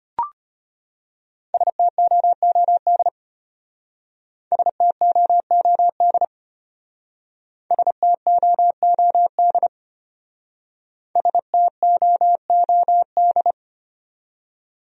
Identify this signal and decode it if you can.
Morse code, heard as a keyed tone, about 33 wpm, 700 Hz. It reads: STOOD STOOD STOOD STOOD